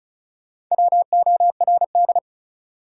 Morse code, heard as a keyed tone, about 35 wpm, 700 Hz. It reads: WORD
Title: Sentences from Top 300 Words - New Word Drill - 35wpm